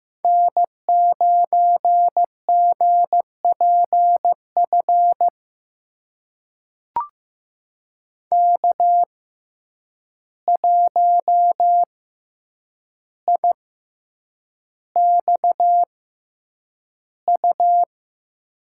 Morse code, heard as a keyed tone, about 15 wpm, spaced as for 8 wpm, 700 Hz. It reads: N9GPF K1IXU
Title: USA Call Signs - 5 Characters - 6x Character Spacing - 15wpm